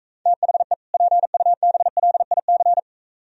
Morse code, heard as a keyed tone, about 44 wpm, 700 Hz. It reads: THE PUBLIC